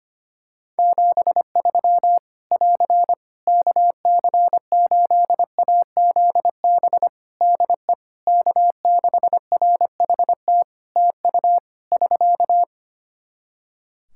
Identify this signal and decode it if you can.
Morse code, heard as a keyed tone, about 25 wpm, 700 Hz. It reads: 73 <AR> KC8AZB DE K6RHT TU <SK>
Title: Practice QSO - Exchange 4 - Ending - 25wpm